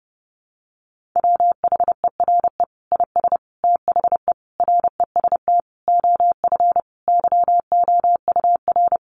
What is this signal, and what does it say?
Morse code, heard as a keyed tone, about 30 wpm, 700 Hz. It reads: WHERE IS THE REST OF YOUR
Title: Sentences from Top 1500 Words 30wpm